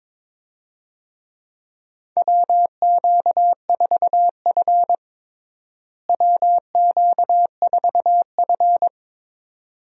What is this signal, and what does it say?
Morse code, heard as a keyed tone, about 22 wpm, 700 Hz. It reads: WQ4F WQ4F
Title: USA Call Signs - 4 Characters - Repeated 2x - 22wpm